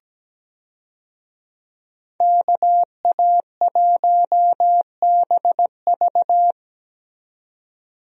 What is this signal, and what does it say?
Morse code, heard as a keyed tone, about 17 wpm, 700 Hz. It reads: KA1BV
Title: USA Call Signs - 5 Characters - 17wpm